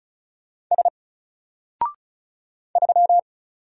Morse code, heard as a keyed tone, about 36 wpm, 700 Hz. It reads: S 3
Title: Single Letter-Number Alt 35wpm